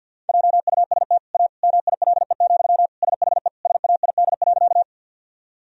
Morse code, heard as a keyed tone, about 52 wpm, 700 Hz. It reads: JUST A MILE, SHE SAID.
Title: Sentences from Top 1000 Words - Review Part 1 - 50wpm